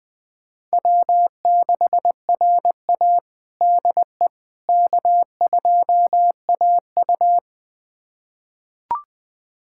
Morse code, heard as a keyed tone, about 20 wpm, 700 Hz. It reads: W6RA DE K2AU
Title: USA Callsign DE Callsign - 4 Characters - 20wpm